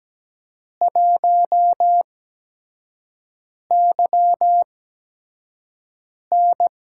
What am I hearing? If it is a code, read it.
Morse code, heard as a keyed tone, about 17 wpm, spaced as for 5 wpm, 700 Hz. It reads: 1YN
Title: USA Call Signs - 5 Characters - 8x Character Spacing - 17wpm